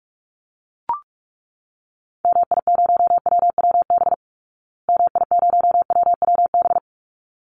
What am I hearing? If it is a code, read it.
Morse code, heard as a keyed tone, about 44 wpm, 700 Hz. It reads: MI0WWB MI0WWB
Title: Call Signs Repeated 3x - 45wpm